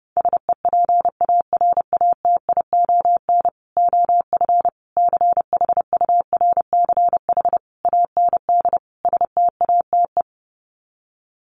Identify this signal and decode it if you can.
Morse code, heard as a keyed tone, about 30 wpm, 700 Hz. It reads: SEPARATION OF CHURCH AND STATE
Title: Sets of 5 Words 30wpm